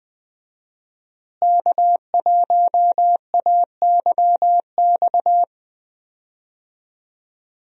Morse code, heard as a keyed tone, about 20 wpm, 700 Hz. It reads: K1AYX
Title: USA Call Signs - 5 Characters - Repeated 3x - 20wpm